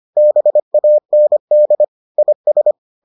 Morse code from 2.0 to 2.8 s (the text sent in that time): IS